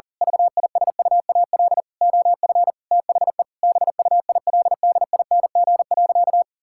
Morse code, heard as a keyed tone, about 40 wpm, 700 Hz. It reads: VISUAL OF THE BUILDING.
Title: Sentences from Top 1000 Words - Review Part 6 - 40wpm